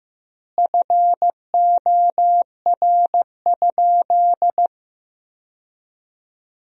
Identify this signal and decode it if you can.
Morse code, heard as a keyed tone, about 15 wpm, 700 Hz. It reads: FOR?
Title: 4x Word-Spacing - Sentences from Top 200 Words - 15wpm